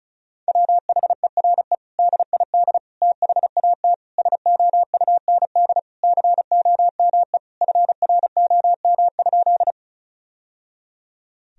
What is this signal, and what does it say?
Morse code, heard as a keyed tone, about 34 wpm, 700 Hz. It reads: WHERE DID THAT SOUND COME FROM?
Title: Sentences from Top 1000 Words - Review Part 1 - 35wpm